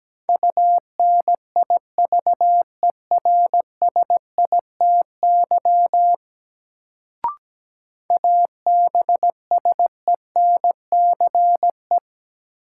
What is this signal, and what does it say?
Morse code, heard as a keyed tone, about 17 wpm, 700 Hz. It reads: UNIVERSITY ABSENCE
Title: Sets of 4 Words - New Word Drill - 17wpm